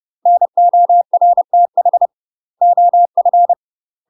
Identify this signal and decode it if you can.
Morse code, heard as a keyed tone, about 30 wpm, 700 Hz. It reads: NORTH OF
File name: UK MASTER QSOs TRAINING / 30WPM / UK MASTER QSOs 0005